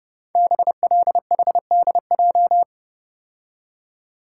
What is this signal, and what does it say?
Morse code, heard as a keyed tone, about 28 wpm, 700 Hz. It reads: BLHDJ
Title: Code Groups - 5 Letters - 30wpm